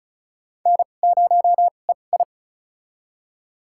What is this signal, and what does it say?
Morse code, heard as a keyed tone, about 34 wpm, spaced as for 25 wpm, 700 Hz. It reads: N0EI
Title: USA Call Signs - 4 Characters - 2x Character Spacing - 35wpm